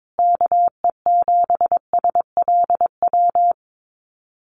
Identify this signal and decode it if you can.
Morse code, heard as a keyed tone, about 22 wpm, 700 Hz. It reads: KE7SLW
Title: USA Call Signs Repeated 2x 22wpm